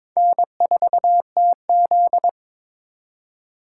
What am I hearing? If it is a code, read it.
Morse code, heard as a keyed tone, about 22 wpm, 700 Hz. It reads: N4TZ
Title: K1USN SST QSO Practice - 22wpm